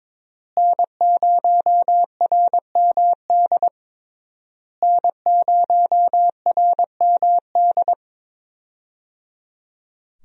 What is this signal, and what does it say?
Morse code, heard as a keyed tone, about 22 wpm, 700 Hz. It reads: N0RMD N0RMD
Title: USA Call Signs - 5 Characters - Repeated 3x - 22wpm